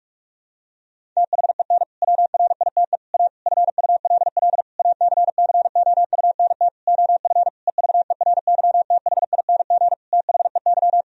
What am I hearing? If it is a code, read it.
Morse code, heard as a keyed tone, about 46 wpm, 700 Hz. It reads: THEN WRITE A FULL ACCOUNT OF EVERYTHING THEY
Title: Sentences from Top 1000 Words - Review Part 3 - 45wpm